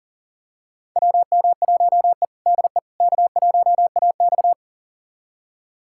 Morse code, heard as a keyed tone, about 40 wpm, 700 Hz. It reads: WM1E DE K1AX
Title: USA Callsign DE Callsign - 4 Characters - 40wpm